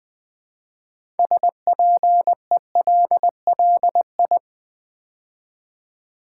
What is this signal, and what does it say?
Morse code, heard as a keyed tone, about 20 wpm, 700 Hz. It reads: SPELLI
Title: Build A Word - 3 Letters At A Time - 8-Letter Words - 20wpm